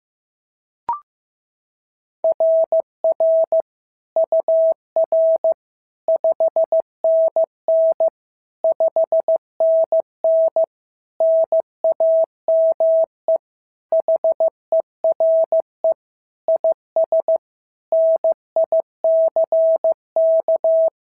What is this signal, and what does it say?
Morse code, heard as a keyed tone, about 14 wpm, 650 Hz. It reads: RR UR 5NN 5NN NAME HERE IS NICK
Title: SOTA QSO Practice 15wpm